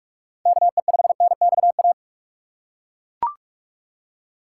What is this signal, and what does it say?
Morse code, heard as a keyed tone, about 41 wpm, 700 Hz. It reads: KE5NXA
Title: USA Call Signs - 6 Characters - 2x Character Spacing - 45wpm